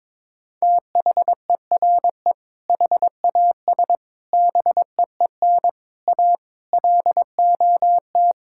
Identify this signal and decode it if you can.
Morse code, heard as a keyed tone, about 22 wpm, 700 Hz. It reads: THERE HAS BEEN A LOT
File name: practice / 22wpm / Sets of 7 Words - 22wpm